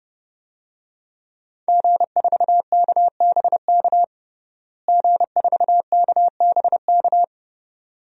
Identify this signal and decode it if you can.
Morse code, heard as a keyed tone, about 30 wpm, 700 Hz. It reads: G4KBK G4KBK
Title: DXCC-100 Call Signs - 5 Characters - Repeated 2x - 30wpm